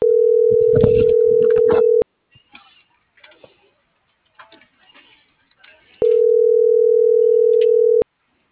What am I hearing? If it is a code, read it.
Ringback tone (North America), 440 + 480 Hz.